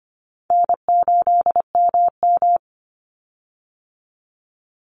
Morse code, heard as a keyed tone, about 25 wpm, 700 Hz. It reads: N8MM
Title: USA Call Signs 25wpm